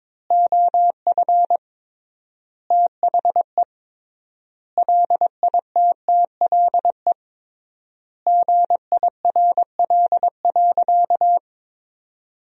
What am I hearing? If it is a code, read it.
Morse code, heard as a keyed tone, about 22 wpm, 700 Hz. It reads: OF THE LITTLE GIRL.
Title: 3x Word-Spacing - Sentences from Top 400 Words - 22wpm